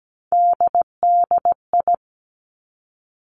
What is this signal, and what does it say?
Morse code, heard as a keyed tone, about 17 wpm, 700 Hz. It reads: DDI
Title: Three Challenging Letter-Number Combinations 17wpm